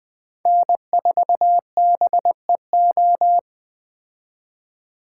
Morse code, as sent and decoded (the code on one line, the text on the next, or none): -. ....- -... ---
N4BEO